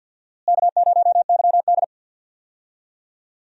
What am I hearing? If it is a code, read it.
Morse code, heard as a keyed tone, about 54 wpm, 700 Hz. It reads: K0YD
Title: USA Call Signs - 4 Characters - Repeated 3x - 50wpm